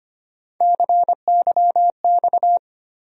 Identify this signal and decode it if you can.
Morse code, heard as a keyed tone, about 25 wpm, 700 Hz. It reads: CYX
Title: USA Call Sign Postfixes - 3 Letters 25wpm